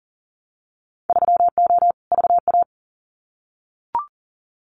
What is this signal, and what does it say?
Morse code, heard as a keyed tone, about 40 wpm, 700 Hz. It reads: 3O VA
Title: Winter Field Day Exchanges 40wpm